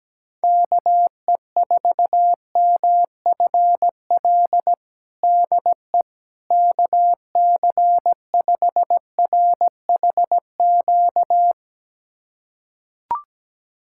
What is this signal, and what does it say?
Morse code, heard as a keyed tone, about 17 wpm, 700 Hz. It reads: KE4MFL DE KC5RHQ
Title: USA Callsign DE Callsign 17wpm